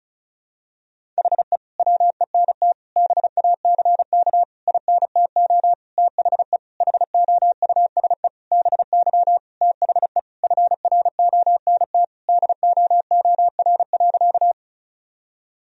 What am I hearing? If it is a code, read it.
Morse code, heard as a keyed tone, about 36 wpm, 700 Hz. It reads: HE WENT BACK INTO THE HOUSE BY THE FRONT DOOR.
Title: Sentences from Top 1000 Words - Review Part 2 - 35wpm